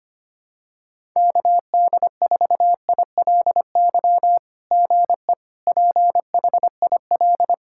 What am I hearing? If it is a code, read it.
Morse code, heard as a keyed tone, about 25 wpm, 700 Hz. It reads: KD4ILY GE PHIL
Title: SOTA QSO Practice 25wpm